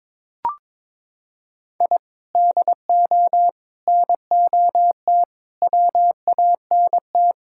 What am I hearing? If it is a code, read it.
Morse code, heard as a keyed tone, about 22 wpm, 700 Hz. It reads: I DO NOT WANT
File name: practice / 22wpm / Sentences with 4-Word Phrases - 22wpm